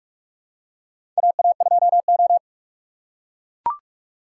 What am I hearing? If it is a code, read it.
Morse code, heard as a keyed tone, about 42 wpm, 700 Hz. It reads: AA2O
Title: USA Call Signs - 4 Characters - 2x Character Spacing - 45wpm